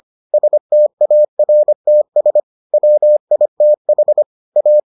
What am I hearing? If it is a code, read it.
Morse code, heard as a keyed tone, about 25 wpm, 600 Hz. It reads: STARTS WITH A